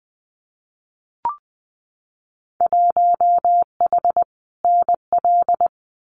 Morse code, heard as a keyed tone, about 20 wpm, 700 Hz. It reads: 1H NL
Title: Winter Field Day Exchanges 20wpm